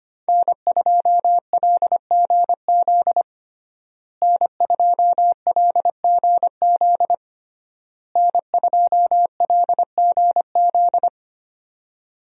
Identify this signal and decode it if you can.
Morse code, heard as a keyed tone, about 25 wpm, 700 Hz. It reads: N2LGZ N2LGZ N2LGZ
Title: USA Call Signs - 5 Characters - Repeated 3x - 25wpm